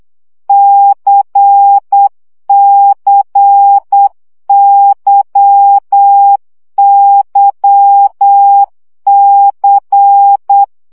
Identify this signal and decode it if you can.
Morse code, heard as a keyed tone, about 8 wpm, 800 Hz. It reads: CCYYC
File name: CW gyakorlas k c y 7 baud